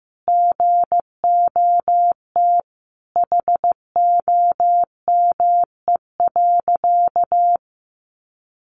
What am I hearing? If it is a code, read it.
Morse code, heard as a keyed tone, about 15 wpm, 700 Hz. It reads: GOT HOME.
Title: Sentences from Top 1400 Words 15wpm